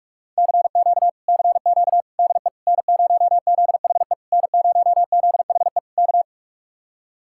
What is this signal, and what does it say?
Morse code, heard as a keyed tone, about 59 wpm, spaced as for 52 wpm, 700 Hz. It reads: CQ CQ DE N0ZHE N0ZHE K
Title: Speed-Racing Common QSO Phrases - 45wpm